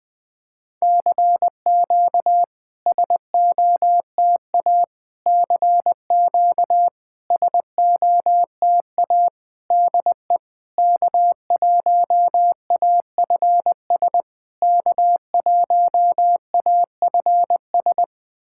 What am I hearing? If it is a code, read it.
Morse code, heard as a keyed tone, about 20 wpm, 690 Hz. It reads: CQ SOTA CQ SOTA DE K1AFS K1AFS
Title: SOTA QSO Practice 20wpm